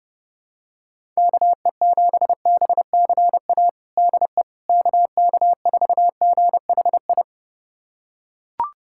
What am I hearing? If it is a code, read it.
Morse code, heard as a keyed tone, about 30 wpm, 700 Hz. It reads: KE7BCA DE KK4GHI
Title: USA Callsign DE Callsign 30wpm